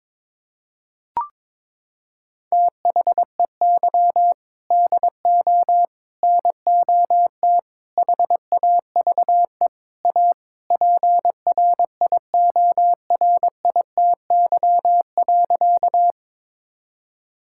Morse code, heard as a keyed tone, about 22 wpm, 700 Hz. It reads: THEY DO NOT HAVE A PRIORITY.
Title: Sentences from Top 1600 Words 22wpm